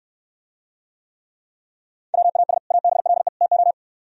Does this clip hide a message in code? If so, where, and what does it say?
Yes: Morse code from 2.1 to 3.7 s, heard as a keyed tone, about 66 wpm, 700 Hz. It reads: WAS ABLE TO